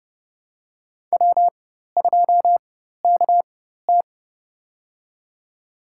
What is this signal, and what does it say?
Morse code, heard as a keyed tone, about 30 wpm, spaced as for 14 wpm, 700 Hz. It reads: W2KT
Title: USA Call Signs - 4 Characters - 4x Character Spacing - 30wpm